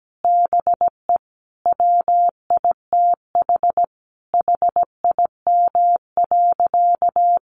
Morse code, heard as a keyed tone, about 17 wpm, 700 Hz. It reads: BE WITH HIM.
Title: Sentences from Top 1500 Words 17wpm